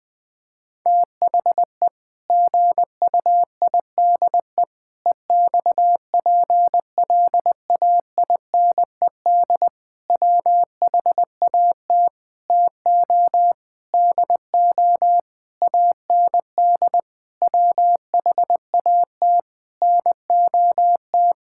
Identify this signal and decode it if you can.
Morse code, heard as a keyed tone, about 20 wpm, 700 Hz. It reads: THE GUIDE EXPLAINED WHAT TO DO AND WHAT NOT
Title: Sentences with 8-Word Phrases - 20wpm